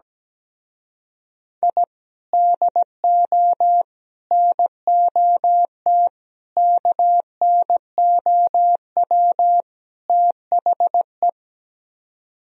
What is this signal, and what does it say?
Morse code, heard as a keyed tone, about 17 wpm, 700 Hz. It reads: I DO NOT KNOW THE
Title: Speed-Racing - Sets of 5 Words 17wpm